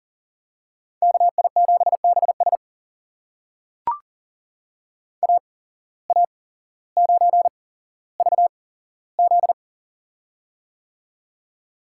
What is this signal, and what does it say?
Morse code, heard as a keyed tone, about 40 wpm, spaced as for 17 wpm, 700 Hz. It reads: KI7BS AA9VZ
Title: USA Call Signs - 5 Characters - 8x Character Spacing - 40wpm